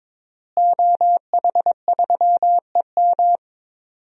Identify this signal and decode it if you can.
Morse code, heard as a keyed tone, about 22 wpm, 700 Hz. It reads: OH3EM
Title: DXCC-100 Call Signs - 5 Characters - Repeated 3x - 22wpm